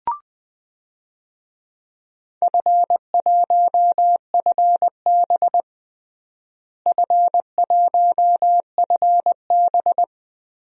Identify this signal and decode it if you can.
Morse code, heard as a keyed tone, about 20 wpm, 700 Hz. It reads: F1FB F1FB
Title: DXCC-100 Call Signs - 4 Characters - Repeated 3x - 20wpm